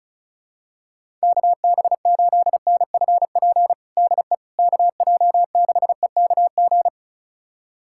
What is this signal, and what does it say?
Morse code, heard as a keyed tone, about 34 wpm, 700 Hz. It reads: KB8NFP DE KJ6EKG
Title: USA Callsign DE Callsign - 6 Characters - 35wpm